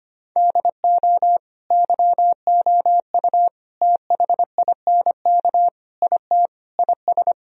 Morse code, heard as a keyed tone, about 25 wpm, 700 Hz. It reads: DO YOU THINK IT IS
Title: Sentences from Top 1000 Words - Review Part 1 - 25wpm